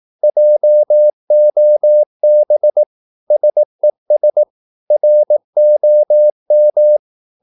Morse code, heard as a keyed tone, about 18 wpm, 600 Hz. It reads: JOB SES ROM
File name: Korte ord DK 18wpm